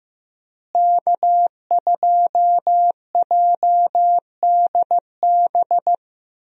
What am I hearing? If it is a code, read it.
Morse code, heard as a keyed tone, about 16 wpm, 700 Hz. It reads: K2JDB
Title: USA Call Signs - 5 Characters - Repeated 3x - 15wpm